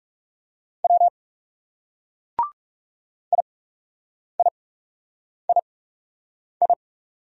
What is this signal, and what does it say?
Morse code, heard as a keyed tone, about 38 wpm, spaced as for 8 wpm, 700 Hz. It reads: W IIII